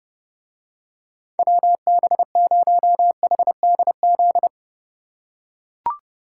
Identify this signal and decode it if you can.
Morse code, heard as a keyed tone, about 30 wpm, 700 Hz. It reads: WB0HDZ